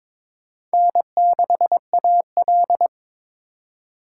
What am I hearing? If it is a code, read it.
Morse code, heard as a keyed tone, about 22 wpm, 700 Hz. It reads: N6AL
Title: USA Call Signs - 4 Characters - 4x Character Spacing - 22wpm